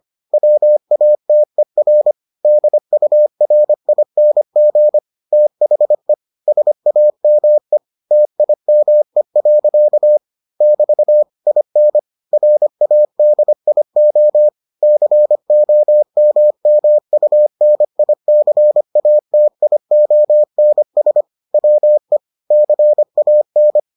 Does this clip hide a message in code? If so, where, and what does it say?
Yes: Morse code from 0.3 to 23.8 s, heard as a keyed tone, about 25 wpm, 600 Hz. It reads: WATER DURING THE SAME TIME. <BT> IN RADIO COMMUNICATIONS WE CAN